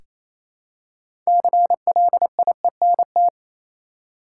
Morse code, heard as a keyed tone, about 28 wpm, 700 Hz. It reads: CLIENT
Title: Six Letter Words - 28wpm